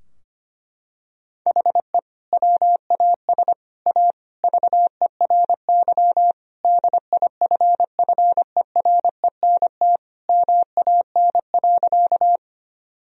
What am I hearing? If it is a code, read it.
Morse code, heard as a keyed tone, about 25 wpm, 700 Hz. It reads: HE WAS A VERY DIFFERENT MAN.
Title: Sentences from Top 1000 Words - Review Part 3 - 25wpm